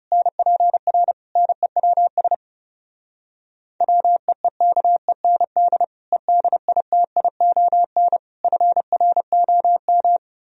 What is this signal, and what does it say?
Morse code, heard as a keyed tone, about 34 wpm, 700 Hz. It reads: NPR NEWS WEEKEND EDITION FROM
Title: Speed-Racing - Sets of 5 Words 30wpm